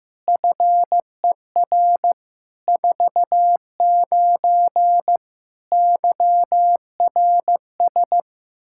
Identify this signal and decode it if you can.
Morse code, heard as a keyed tone, about 15 wpm, 700 Hz. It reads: FER 49 YRS
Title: Practice QSO - Exchange 3 - More Info - 15wpm